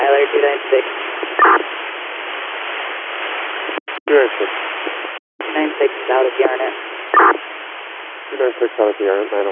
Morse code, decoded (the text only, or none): none